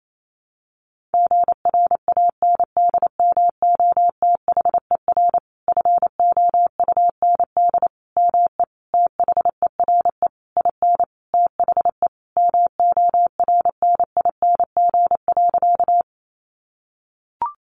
Morse code, heard as a keyed tone, about 28 wpm, 700 Hz. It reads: GRANDMOTHER FOUND ME THERE IN THE MORNING.
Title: Sentences from Top 1200 Words 28wpm